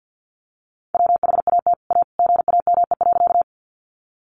Morse code, heard as a keyed tone, about 50 wpm, 700 Hz. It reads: WHAT A GAME.
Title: Sentences from Top 500 Words 50wpm